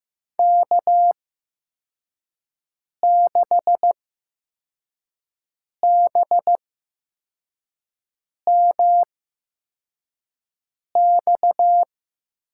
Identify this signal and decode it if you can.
Morse code, heard as a keyed tone, about 15 wpm, spaced as for 4 wpm, 700 Hz. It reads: K6BMX
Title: USA Call Signs - 5 Characters - 8x Character Spacing - 15wpm